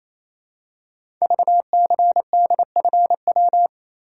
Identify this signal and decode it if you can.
Morse code, heard as a keyed tone, about 28 wpm, 700 Hz. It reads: VCDFW
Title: Code Groups - 5 Letters - 28wpm